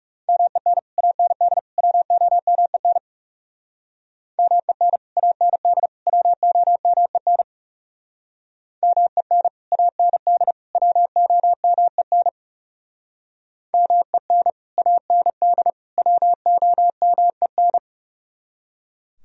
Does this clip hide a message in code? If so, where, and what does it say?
Yes: Morse code from 0.3 to 17.8 s, heard as a keyed tone, about 40 wpm, 700 Hz. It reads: MEN AND WOMEN MEN AND WOMEN MEN AND WOMEN MEN AND WOMEN